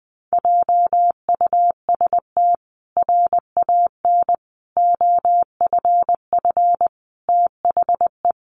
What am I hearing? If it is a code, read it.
Morse code, heard as a keyed tone, about 19 wpm, 700 Hz. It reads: JUST RAN OFF THE
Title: A Sound Of Thunder 20wpm